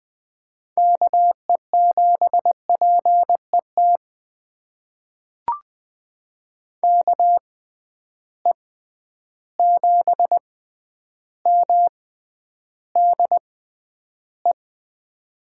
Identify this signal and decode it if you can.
Morse code, heard as a keyed tone, about 20 wpm, spaced as for 10 wpm, 700 Hz. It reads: KE7PET KE7MDE
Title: USA Call Signs - 6 Characters - 6x Character Spacing - 20wpm